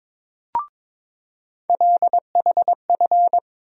Morse code, heard as a keyed tone, about 22 wpm, 700 Hz. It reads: LHF
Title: USA Call Sign Postfixes - 3 Letters 22wpm